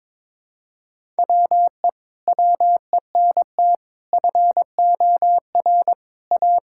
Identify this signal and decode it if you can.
Morse code, heard as a keyed tone, about 22 wpm, 700 Hz. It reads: WE WENT FOR A
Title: Sentences from Top 1000 Words - Review Part 6 - 22wpm